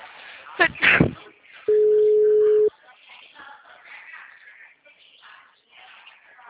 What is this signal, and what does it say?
Ringback tone (Europe, 425 Hz), with speech.
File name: out- -2017- - - .29280